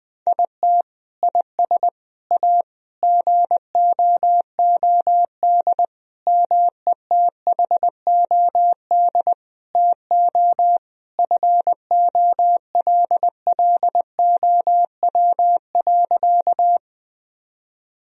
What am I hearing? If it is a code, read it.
Morse code, heard as a keyed tone, about 19 wpm, 700 Hz. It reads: IT IS A GOOD METHOD TO FOLLOW.
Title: Sentences from Top 1000 Words - Review Part 3 - 20wpm